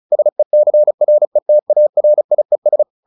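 Morse code, heard as a keyed tone, about 34 wpm, 600 Hz. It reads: SECRETARIES
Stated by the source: Morse practice at 35 wpm from CW med Gnister, sending Ken Follett's novel Never in English.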